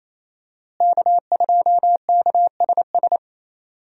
Morse code, heard as a keyed tone, about 27 wpm, 700 Hz. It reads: K2KSS
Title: USA Call Signs - 5 Characters - Repeated 2x - 28wpm